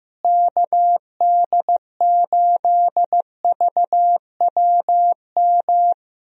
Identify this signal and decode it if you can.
Morse code, heard as a keyed tone, about 15 wpm, 700 Hz. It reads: KD8VWM
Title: USA Call Signs - 6 Characters - Repeated 2x - 15wpm